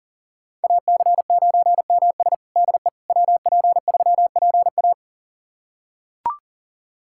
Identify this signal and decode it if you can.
Morse code, heard as a keyed tone, about 37 wpm, 700 Hz. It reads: AC9MS DE WP3PA
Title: USA Callsign DE Callsign - 5 Characters - 40wpm